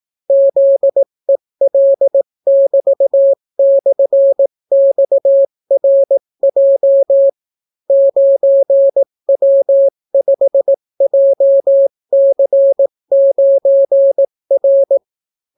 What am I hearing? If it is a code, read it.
Morse code, heard as a keyed tone, about 18 wpm, 550 Hz. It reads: ZEL<BT>/XRJ 9W5JC9R